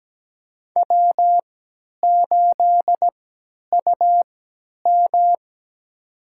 Morse code, heard as a keyed tone, about 17 wpm, spaced as for 10 wpm, 700 Hz. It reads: W8UM
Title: USA Call Signs - 3x Character Spacing - 17wpm